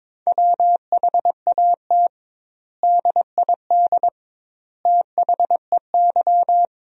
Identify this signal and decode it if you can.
Morse code, heard as a keyed tone, about 22 wpm, 700 Hz. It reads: WHAT DID THEY
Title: 2x Word-Spacing - Sentences from Top 100 Words - 22wpm